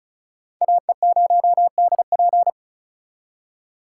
Morse code, heard as a keyed tone, about 46 wpm, 700 Hz. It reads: AE0DP